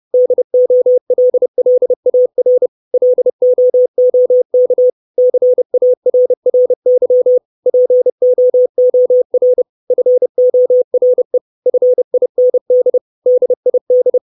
Morse code, heard as a keyed tone, about 30 wpm, 500 Hz. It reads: DOLLAR LOOK CARRY POOR FORE FIND DID
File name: 632 Words 30WPM